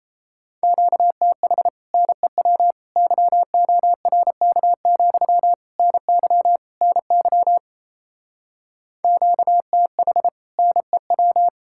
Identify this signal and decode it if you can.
Morse code, heard as a keyed tone, about 37 wpm, 700 Hz. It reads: QTH NEW YORK, NY NY QTH NEW